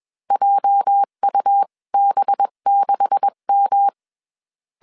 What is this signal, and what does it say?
Morse code, heard as a keyed tone, about 21 wpm, spaced as for 19 wpm, 800 Hz. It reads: JUB6M
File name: CW / DET / 1 9 1500C S B 20wpm 2313